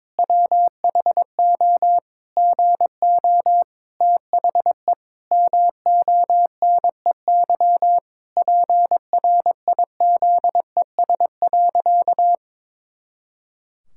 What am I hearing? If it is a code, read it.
Morse code, heard as a keyed tone, about 22 wpm, 700 Hz. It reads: WHO GO THE MONEY PRIZES.
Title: Pangram Challenge v2 - 22wpm